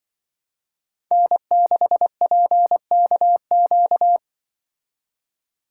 Morse code, heard as a keyed tone, about 25 wpm, 700 Hz. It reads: N6PKQ
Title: Speed Racing - USA 1x3 Call Signs 20wpm